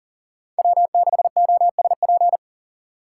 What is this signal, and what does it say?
Morse code, heard as a keyed tone, about 40 wpm, 700 Hz. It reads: W6OSP
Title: K1USN SST QSO Practice - 40wpm